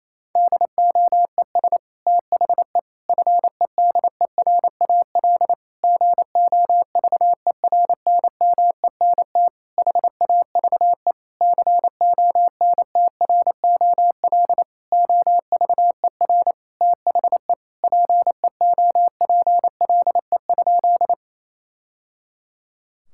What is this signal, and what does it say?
Morse code, heard as a keyed tone, about 28 wpm, 700 Hz. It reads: DOES THE FEDERAL GOVERNMENT HAVE CONTROL OVER THE PEOPLE?
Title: Sentences from Top 1000 Words - Review Part 5 - 28wpm